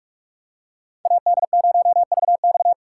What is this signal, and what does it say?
Morse code, heard as a keyed tone, about 43 wpm, 700 Hz. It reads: AD0VX
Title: USA Call Signs - 2x Character Spacing - 45wpm